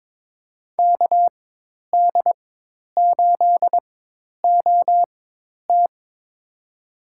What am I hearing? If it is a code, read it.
Morse code, heard as a keyed tone, about 22 wpm, spaced as for 10 wpm, 700 Hz. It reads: KD8OT
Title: USA Call Signs - 4x Character Spacing - 22wpm